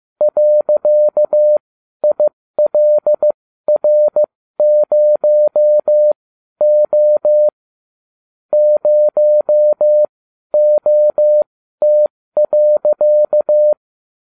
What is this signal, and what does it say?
Morse code, heard as a keyed tone, about 15 wpm, spaced as for 12 wpm, 600 Hz. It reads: .ILR0O 0OT.